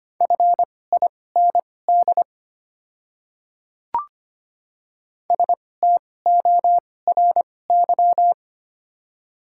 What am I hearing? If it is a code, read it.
Morse code, heard as a keyed tone, about 25 wpm, spaced as for 18 wpm, 700 Hz. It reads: FIND STORY
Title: 2x Character Spacing - Top 300 Words - 25wpm